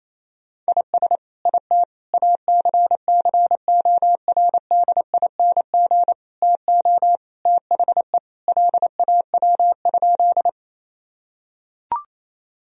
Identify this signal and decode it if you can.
Morse code, heard as a keyed tone, about 28 wpm, 700 Hz. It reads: IS IT ACCORDING TO THE LAW?
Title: Sentences from Top 1000 Words - Review Part 4 - 28wpm